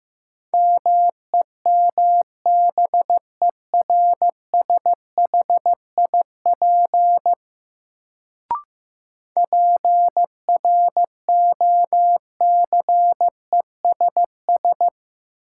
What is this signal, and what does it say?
Morse code, heard as a keyed tone, about 15 wpm, 700 Hz. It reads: MEMBERSHIP PROCESS
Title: Sentences from Top 1400 Words - New Word Drill Alt - 15wpm